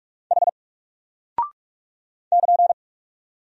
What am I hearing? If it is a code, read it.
Morse code, heard as a keyed tone, about 43 wpm, 700 Hz. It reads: H <KN>
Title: Single Character (Letter:Number:Punctuation:Prosign) Alt 45wpm